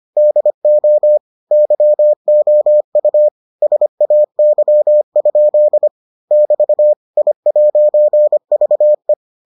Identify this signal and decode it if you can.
Morse code, heard as a keyed tone, about 25 wpm, 600 Hz. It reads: DO YOU SAY? <BT> I'VE